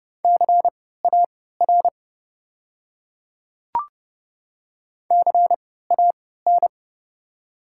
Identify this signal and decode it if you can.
Morse code, heard as a keyed tone, about 30 wpm, spaced as for 17 wpm, 700 Hz. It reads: CAR CAN
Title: 3x Character Spacing - Top 300 Words - 30wpm